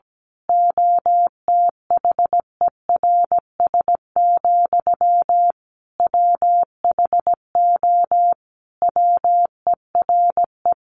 Morse code, heard as a keyed tone, about 17 wpm, 700 Hz. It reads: OTHERS, WHO WERE
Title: A Sound Of Thunder 17wpm